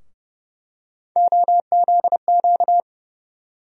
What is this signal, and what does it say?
Morse code, heard as a keyed tone, about 30 wpm, 700 Hz. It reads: OZQ
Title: USA Call Sign Postfixes - 3 Letters 30wpm